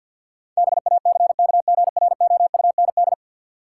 Morse code, heard as a keyed tone, about 46 wpm, 700 Hz. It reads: BACKGROUND
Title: Sentences from Top 1300 Words - New Word Drill - 50wpm